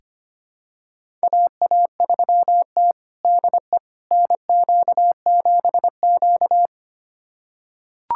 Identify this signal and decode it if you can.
Morse code, heard as a keyed tone, about 25 wpm, 700 Hz. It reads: AA3T DE NQ7Q
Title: USA Callsign DE Callsign - 4 Characters - 25wpm